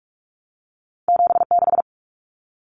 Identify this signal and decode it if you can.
Morse code, heard as a keyed tone, about 46 wpm, 700 Hz. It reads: 76
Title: Numbers - 2-Digit 45wpm